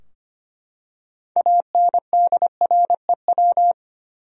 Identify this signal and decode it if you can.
Morse code, heard as a keyed tone, about 25 wpm, 700 Hz. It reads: ANDREW
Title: USA Common Names - 25wpm